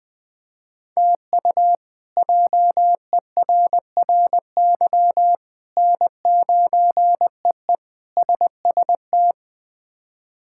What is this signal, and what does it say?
Morse code, heard as a keyed tone, about 20 wpm, 700 Hz. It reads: TU JERRY N9EE SST
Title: K1USN SST QSO Practice - 20wpm